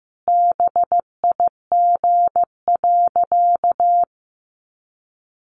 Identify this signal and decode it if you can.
Morse code, heard as a keyed tone, about 15 wpm, 700 Hz. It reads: BIG.
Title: Sentences from Top 600 Words 15wpm